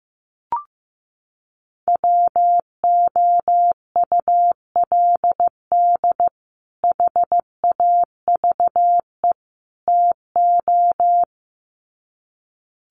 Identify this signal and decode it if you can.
Morse code, heard as a keyed tone, about 15 wpm, 700 Hz. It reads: WOULD HAVE TO
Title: Sets of 3 Words 15wpm